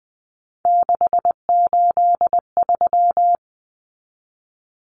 Morse code, heard as a keyed tone, about 20 wpm, 700 Hz. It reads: 683
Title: Common Cut Numbers - 3-Digit 20wpm